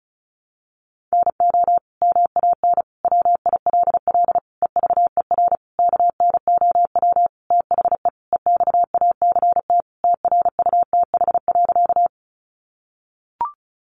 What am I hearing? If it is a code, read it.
Morse code, heard as a keyed tone, about 34 wpm, 700 Hz. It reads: NO MAN WILL EVER KNOW THE EXACT TRUTH.
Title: Sentences from Top 1300 Words 35wpm